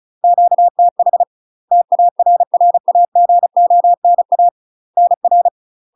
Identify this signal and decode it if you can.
Morse code, heard as a keyed tone, about 35 wpm, 700 Hz. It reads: QTH TARRAGONA NR
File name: UK MASTER QSOs TRAINING / 35WPM / UK MASTER QSOs 0031